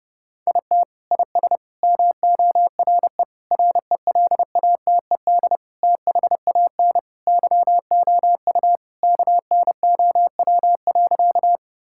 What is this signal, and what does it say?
Morse code, heard as a keyed tone, about 32 wpm, 700 Hz. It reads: IT IS MORE RELATED THAN YOU KNOW.
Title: Sentences from Top 1000 Words - Review Part 5 - 30wpm